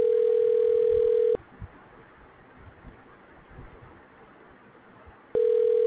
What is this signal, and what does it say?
Ringback tone (North America), 440 + 480 Hz.